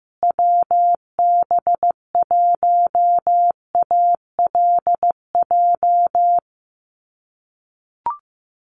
Morse code, heard as a keyed tone, about 15 wpm, 700 Hz. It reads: WB1ALJ
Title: USA Call Signs Repeated 2x 15wpm